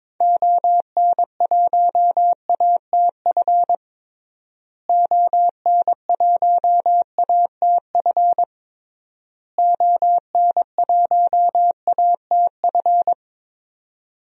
Morse code, heard as a keyed tone, about 22 wpm, 700 Hz. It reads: ON1ATF ON1ATF ON1ATF
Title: DXCC-100 Call Signs - 6 Characters - Repeated 3x - 22wpm